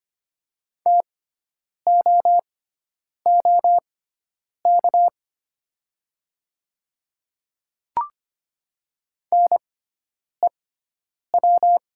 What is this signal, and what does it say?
Morse code, heard as a keyed tone, about 25 wpm, spaced as for 9 wpm, 700 Hz. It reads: TOOK NEW
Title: 6x Character Spacing - Top 500 Words - 25wpm